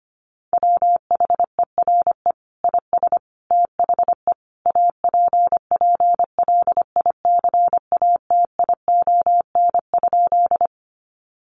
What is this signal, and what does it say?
Morse code, heard as a keyed tone, about 25 wpm, 700 Hz. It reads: WHERE IS THE APPLICATION?
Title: Sentences from Top 1200 Words 25wpm